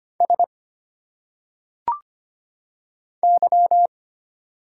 Morse code, heard as a keyed tone, about 25 wpm, 700 Hz. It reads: S Y
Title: Single Letters - Phonetic Pronunciation 25wpm